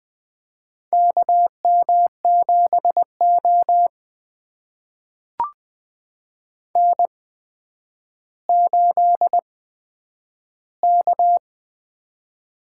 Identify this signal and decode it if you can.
Morse code, heard as a keyed tone, about 20 wpm, spaced as for 10 wpm, 700 Hz. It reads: KM7O N8K
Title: USA Call Signs - 4 Characters - 8x Character Spacing - 20wpm